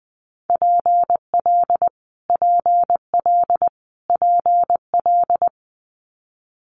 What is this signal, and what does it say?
Morse code, heard as a keyed tone, about 20 wpm, 700 Hz. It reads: PL PL PL
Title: Multi-letter Phonemes Repeated 3x 20wpm